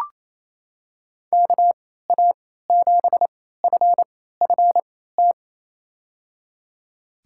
Morse code, heard as a keyed tone, about 28 wpm, spaced as for 16 wpm, 700 Hz. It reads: KA7FFT
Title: USA Call Signs - 6 Characters - 3x Character Spacing - 28wpm